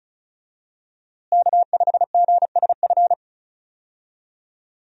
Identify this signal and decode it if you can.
Morse code, heard as a keyed tone, about 34 wpm, 700 Hz. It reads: K5GSF
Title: USA Call Signs - 5 Characters - Repeated 3x - 35wpm